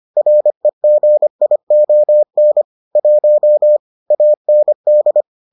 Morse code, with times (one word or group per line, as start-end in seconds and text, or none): 0.2-2.6 REGION
3.0-3.8 1
4.1-5.2 AND